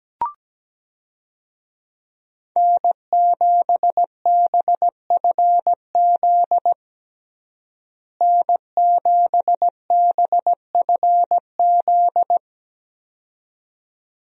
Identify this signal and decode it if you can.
Morse code, heard as a keyed tone, about 17 wpm, 700 Hz. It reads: N7BFZ N7BFZ
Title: USA Call Signs - 5 Characters - Repeated 2x - 17wpm